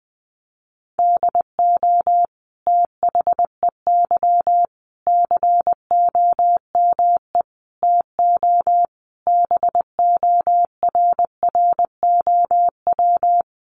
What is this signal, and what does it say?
Morse code, heard as a keyed tone, about 20 wpm, 700 Hz. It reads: DO THEY COME TO BORROW
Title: Sentences from Top 1500 Words 20wpm